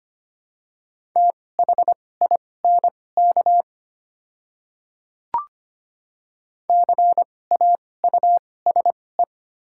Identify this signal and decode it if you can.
Morse code, heard as a keyed tone, about 25 wpm, spaced as for 18 wpm, 700 Hz. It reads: THINK CAUSE
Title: 2x Character Spacing - Top 200 Words - 25wpm